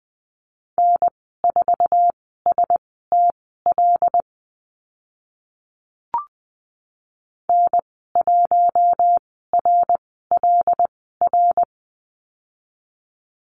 Morse code, heard as a keyed tone, about 20 wpm, spaced as for 14 wpm, 700 Hz. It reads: N4STL N1RLR
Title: 082 Lesson 14g - Number 2 - Call Signs 20wpm